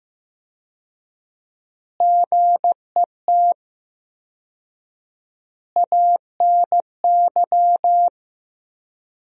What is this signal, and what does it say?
Morse code, heard as a keyed tone, about 16 wpm, 700 Hz. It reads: GET ANY
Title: 4x Word-Spacing - Sentences from Top 400 Words - 15wpm